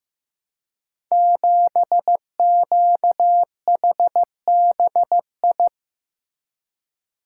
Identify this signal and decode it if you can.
Morse code, heard as a keyed tone, about 15 wpm, 700 Hz. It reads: 7QHBI
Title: Code Groups - 5 Letter:Numbers - 15wpm